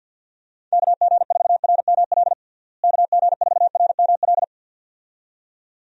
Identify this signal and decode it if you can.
Morse code, heard as a keyed tone, about 53 wpm, 700 Hz. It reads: KG4RML KG4RML
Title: USA Call Signs - 6 Characters - Repeated 2x - 50wpm